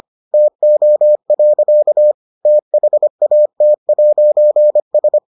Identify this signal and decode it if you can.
Morse code, heard as a keyed tone, about 25 wpm, 600 Hz. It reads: TO. THAT'S